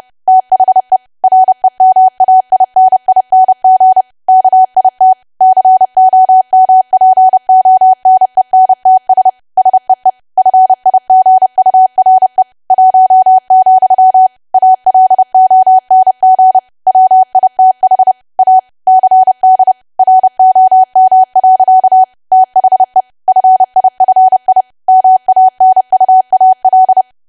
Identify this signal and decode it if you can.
Morse code, heard as a keyed tone, about 30 wpm, 750 Hz. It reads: THE REMAINING KIT COMPONENTS SEE FIGURE 1, ALONG WITH A CD ROM. THE FIFI MANUAL
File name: cwa / W1AW-Level3 / 30WPM